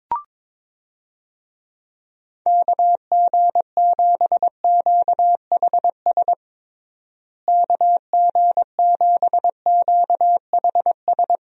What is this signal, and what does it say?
Morse code, heard as a keyed tone, about 22 wpm, 700 Hz. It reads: KG7QHS KG7QHS
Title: USA Call Signs - 6 Characters - Repeated 2x - 22wpm